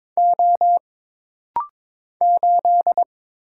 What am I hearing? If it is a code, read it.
Morse code, heard as a keyed tone, about 22 wpm, 700 Hz. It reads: O 8
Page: This rapid-fire timing is geared to push your ICR (Instant Character Recognition) skills.